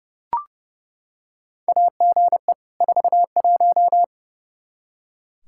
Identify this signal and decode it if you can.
Morse code, heard as a keyed tone, about 30 wpm, 700 Hz. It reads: AGE 41
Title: K1USN SST QSO Practice 30wpm